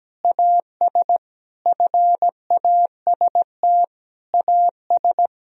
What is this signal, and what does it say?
Morse code, heard as a keyed tone, about 17 wpm, 700 Hz. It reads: AS FAST AS
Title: Sentences from Top 1000 Words - Review Part 2 - 17wpm